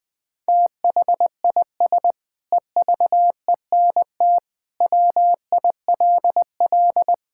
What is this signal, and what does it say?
Morse code, heard as a keyed tone, about 20 wpm, 700 Hz. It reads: THIS EVENT WILL
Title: Sentences with 9-Word Phrases - Spoken Beginning - 20wpm